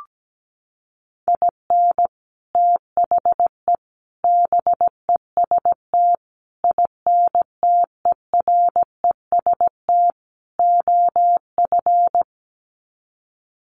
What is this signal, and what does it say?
Morse code, heard as a keyed tone, about 17 wpm, 700 Hz. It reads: IN THE BEST INTEREST OF
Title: Sets of 5 Words Encore 17wpm